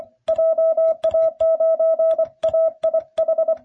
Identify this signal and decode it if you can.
Morse code, heard as a keyed tone, about 24 wpm, 660 Hz. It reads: JA9AIH